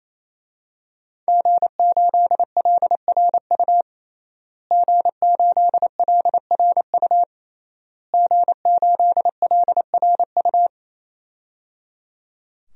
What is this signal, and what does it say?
Morse code, heard as a keyed tone, about 27 wpm, 700 Hz. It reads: G8LRU G8LRU G8LRU
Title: DXCC-100 Call Signs - 5 Characters - Repeated 3x - 28wpm